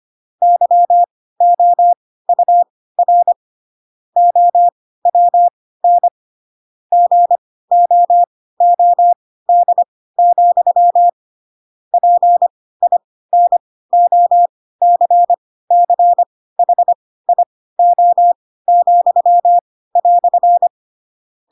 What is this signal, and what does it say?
Morse code, heard as a keyed tone, about 25 wpm, spaced as for 16 wpm, 700 Hz. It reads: YOUR OWN GOOD, PINOCCHIO,"
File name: Pinocchio / Pinocchio ch4 10 25 ik1hge